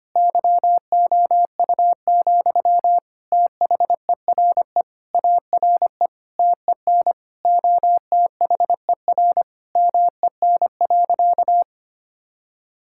Morse code, heard as a keyed tone, about 25 wpm, 700 Hz. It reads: YOU, THERE ARE TEN OTHER MEN.
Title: Sentences from Top 1000 Words - Review Part 6 - 25wpm